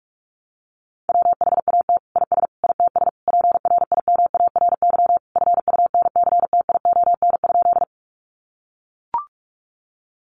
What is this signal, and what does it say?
Morse code, heard as a keyed tone, about 44 wpm, 700 Hz. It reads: WHAT IS ITS PRIMARY FUNCTION?
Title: Sentences from Top 1100 Words 45wpm